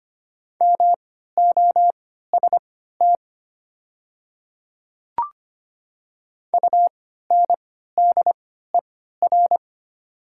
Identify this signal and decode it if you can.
Morse code, heard as a keyed tone, about 25 wpm, spaced as for 14 wpm, 700 Hz. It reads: MOST UNDER
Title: 3x Character Spacing - Top 200 Words - 25wpm